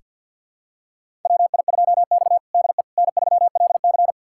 Morse code, heard as a keyed tone, about 46 wpm, 700 Hz. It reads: WI2X DE N3LC
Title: USA Callsign DE Callsign - 4 Characters - 50wpm